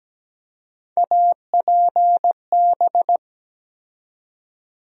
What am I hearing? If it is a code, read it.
Morse code, heard as a keyed tone, about 17 wpm, 700 Hz. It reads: APB
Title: USA Call Sign Postfixes - 3 Letters 17wpm